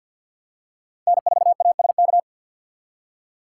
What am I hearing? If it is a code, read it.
Morse code, heard as a keyed tone, about 53 wpm, 700 Hz. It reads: N4ASK
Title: USA Call Signs - 5 Characters - 2x Character Spacing - 50wpm